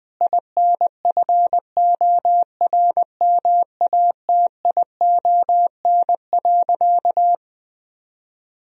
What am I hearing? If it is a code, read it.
Morse code, heard as a keyed tone, about 20 wpm, 700 Hz. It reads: INFORMATION.
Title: Sentences from Top 1000 Words - Review Part 3 - 20wpm